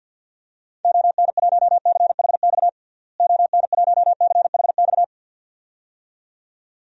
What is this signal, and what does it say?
Morse code, heard as a keyed tone, about 49 wpm, 700 Hz. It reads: ON1CHX ON1CHX
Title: DXCC-100 Call Signs - 6 Characters - Repeated 2x - 50wpm